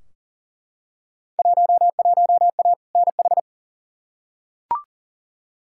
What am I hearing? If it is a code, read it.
Morse code, heard as a keyed tone, about 40 wpm, 700 Hz. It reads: 11A NH
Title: Field Day Exchanges Encore - 40wpm